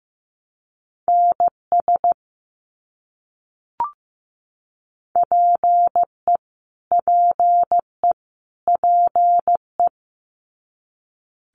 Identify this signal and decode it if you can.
Morse code, heard as a keyed tone, about 15 wpm, 700 Hz. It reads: NS PE PE PE
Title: Canadian Provinces and Territories - Repeated 3x - 15wpm